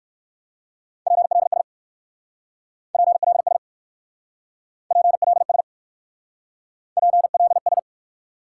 Morse code, heard as a keyed tone, about 60 wpm, 700 Hz. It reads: PLS PLS PLS PLS